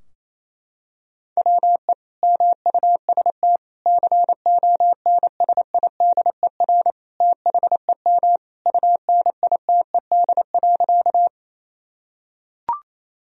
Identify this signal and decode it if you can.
Morse code, heard as a keyed tone, about 28 wpm, 700 Hz. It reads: WE MUST CONSIDER THEM UNITED.
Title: Sentences from Top 1000 Words - Review Part 4 - 28wpm